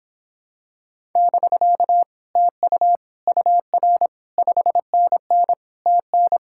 Morse code, heard as a keyed tone, about 26 wpm, 700 Hz. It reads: <BK> TU UR 5NN TN